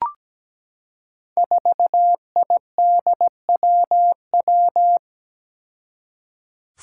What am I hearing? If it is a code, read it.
Morse code, heard as a keyed tone, about 17 wpm, 700 Hz. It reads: 4IDWW
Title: Five Challenging Letter-Number Combinations 17wpm